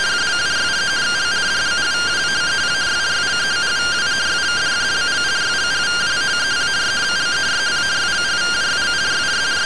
Two-tone teleprinter signal, about 170 Hz shift.